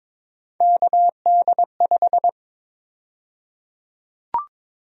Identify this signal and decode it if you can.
Morse code, heard as a keyed tone, about 21 wpm, 700 Hz. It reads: KD5
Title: USA Call Sign Prefixes - 22wpm